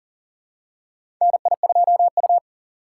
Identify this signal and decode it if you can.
Morse code, heard as a keyed tone, about 40 wpm, 700 Hz. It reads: NI2U
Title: USA Call Signs - 4 Characters - 4x Character Spacing - 40wpm